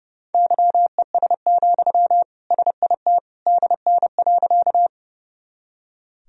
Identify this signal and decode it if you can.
Morse code, heard as a keyed tone, about 32 wpm, 700 Hz. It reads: YES, SIT DN.
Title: Sentences from Top 500 Words and Abbreviations 30wpm